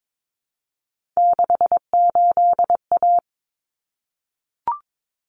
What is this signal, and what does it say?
Morse code, heard as a keyed tone, about 22 wpm, 700 Hz. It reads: 68A
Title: Common Cut Numbers - 3-Digit 22wpm